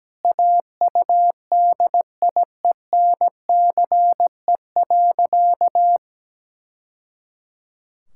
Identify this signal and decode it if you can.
Morse code, heard as a keyed tone, about 17 wpm, 700 Hz. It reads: AUDIENCE.
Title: Sentences from Top 1000 Words - Review Part 5 - 17wpm